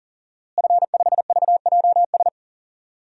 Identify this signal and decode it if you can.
Morse code, heard as a keyed tone, about 40 wpm, 700 Hz. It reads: F5VJS
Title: DXCC-100 Call Signs - 5 Characters - Repeated 2x - 40wpm